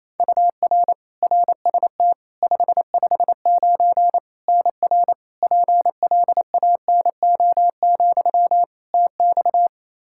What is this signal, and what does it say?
Morse code, heard as a keyed tone, about 28 wpm, 700 Hz. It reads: UR RST 559 NR PLANO, TX
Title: Practice QSO - Exchange 1 - Beginning - 28wpm